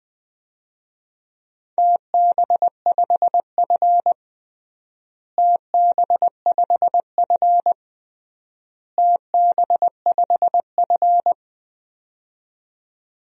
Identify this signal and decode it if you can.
Morse code, heard as a keyed tone, about 21 wpm, 700 Hz. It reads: TB5F TB5F TB5F
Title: DXCC-100 Call Signs - 4 Characters - Repeated 3x - 20wpm